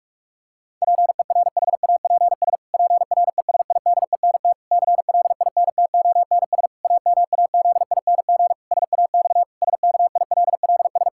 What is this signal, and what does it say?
Morse code, heard as a keyed tone, about 43 wpm, 700 Hz. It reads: PERHAPS PRESIDENT CLINTONS AMAZING SAX SKILLS
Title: Pangram Challenge v2 - 45wpm